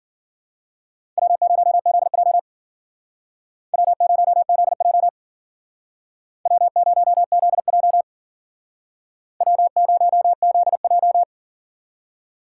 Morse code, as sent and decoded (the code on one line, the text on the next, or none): .-- ----- --.. .--- / .-- ----- --.. .--- / .-- ----- --.. .--- / .-- ----- --.. .---
W0ZJ W0ZJ W0ZJ W0ZJ